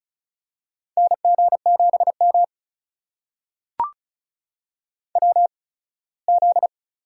Morse code, heard as a keyed tone, about 36 wpm, spaced as for 21 wpm, 700 Hz. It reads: NG7M WZ
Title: USA Call Signs - 4 Characters - 8x Character Spacing - 35wpm